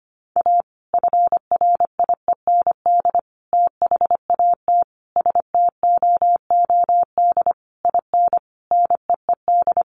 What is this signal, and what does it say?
Morse code, heard as a keyed tone, about 25 wpm, 700 Hz. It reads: A FRIEND THAT STOOD IN NEED
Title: Sentences from Top 1500 Words 25wpm